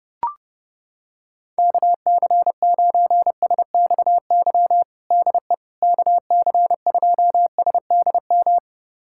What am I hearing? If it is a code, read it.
Morse code, heard as a keyed tone, about 30 wpm, 700 Hz. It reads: KC9SXY DE KC2SDM
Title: USA Callsign DE Callsign - 6 Characters - 30wpm